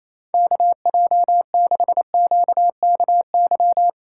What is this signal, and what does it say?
Morse code, heard as a keyed tone, about 28 wpm, 690 Hz. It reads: KJ6QKY
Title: USA Call Signs - 6 Characters - 2x Character Spacing - 28wpm